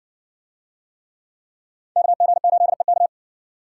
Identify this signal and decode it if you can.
Morse code, heard as a keyed tone, about 53 wpm, 700 Hz. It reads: KG8EK